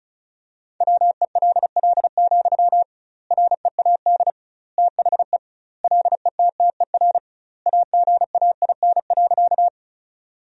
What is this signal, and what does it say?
Morse code, heard as a keyed tone, about 34 wpm, 700 Hz. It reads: WELL, READ THE LETTER AGAIN.
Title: 2x Word-Spacing - Sentences from Top 300 Words - 35wpm